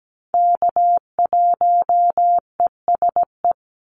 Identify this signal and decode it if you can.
Morse code, heard as a keyed tone, about 17 wpm, 700 Hz. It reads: K1ESE
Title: Call Signs - 17wpm